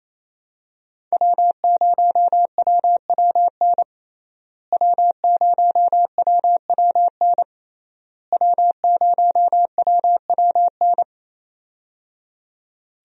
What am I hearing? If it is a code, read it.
Morse code, heard as a keyed tone, about 28 wpm, 700 Hz. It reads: W0WWN W0WWN W0WWN
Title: USA Call Signs - 5 Characters - Repeated 3x - 28wpm